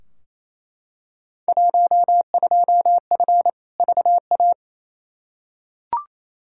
Morse code, heard as a keyed tone, about 28 wpm, 700 Hz. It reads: 12F VA
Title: Field Day Exchanges Encore - 28wpm